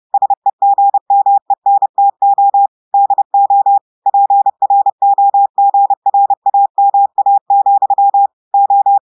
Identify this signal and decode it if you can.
Morse code, heard as a keyed tone, about 30 wpm, 800 Hz. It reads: SEGMENTO DO PROGRAMA, O
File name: audios / portugais / portugais texte en clair 2 vitesse 30 mots minutes